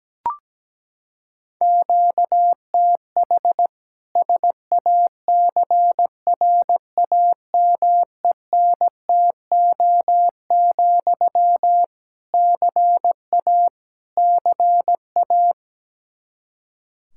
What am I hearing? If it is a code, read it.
Morse code, heard as a keyed tone, about 17 wpm, 700 Hz. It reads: QTH SACRAMENTO, CA CA
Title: Practice QSO - Exchange 1 - Beginning - 17wpm